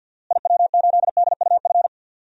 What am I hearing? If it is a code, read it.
Morse code, heard as a keyed tone, about 50 wpm, 700 Hz. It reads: IW8DUF
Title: DXCC-100 Call Signs - 6 Characters - Repeated 2x - 50wpm